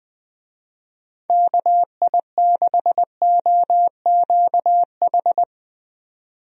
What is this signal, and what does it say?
Morse code, heard as a keyed tone, about 20 wpm, 700 Hz. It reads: KI6OQH
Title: USA Call Signs - 6 Characters - 20wpm